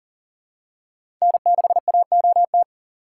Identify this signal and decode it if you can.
Morse code, heard as a keyed tone, about 40 wpm, 700 Hz. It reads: N6AOT
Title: USA Call Signs - 5 Characters - 6x Character Spacing - 40wpm